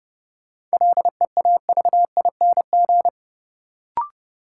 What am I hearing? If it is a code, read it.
Morse code, heard as a keyed tone, about 30 wpm, 700 Hz. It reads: LEAVING
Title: Sentences from Top 1000 Words - Review Part 4 - New Word Drill - 30wpm